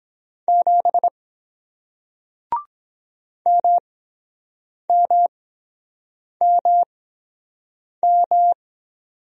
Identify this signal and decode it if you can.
Morse code, heard as a keyed tone, about 23 wpm, spaced as for 7 wpm, 700 Hz. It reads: 7 MMMM